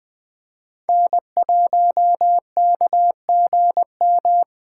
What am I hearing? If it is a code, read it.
Morse code, heard as a keyed tone, about 21 wpm, 700 Hz. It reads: N1KGM
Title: Speed Racing - USA 1x3 Call Signs 17wpm